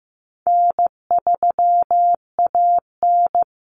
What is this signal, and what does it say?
Morse code, heard as a keyed tone, about 15 wpm, 700 Hz. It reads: N3AN
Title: Common Cut Numbers - 4-Digit 15wpm